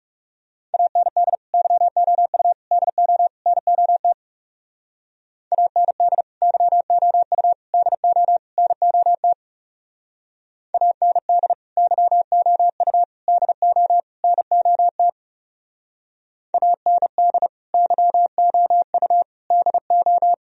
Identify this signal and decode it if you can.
Morse code, heard as a keyed tone, about 36 wpm, 700 Hz. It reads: AND YOU DO NOT AND YOU DO NOT AND YOU DO NOT AND YOU DO
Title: Speed-Racing - Sets of 4 Words 30wpm